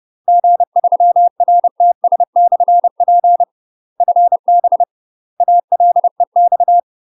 Morse code, heard as a keyed tone, about 30 wpm, 700 Hz. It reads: G3RTS/P FB ALEX
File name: UK MASTER QSOs TRAINING / 30WPM / UK MASTER QSOs 0029